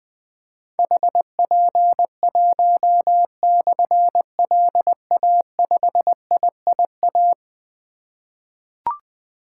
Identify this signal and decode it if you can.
Morse code, heard as a keyed tone, about 20 wpm, 700 Hz. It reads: HP1/LA5IIA
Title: Difficult Call Signs 20wpm